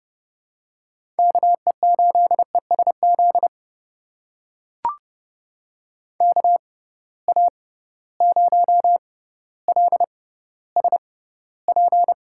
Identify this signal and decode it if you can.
Morse code, heard as a keyed tone, about 30 wpm, spaced as for 15 wpm, 700 Hz. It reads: KE8ESZ KA0LSP
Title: USA Call Signs - 6x Character Spacing - 30wpm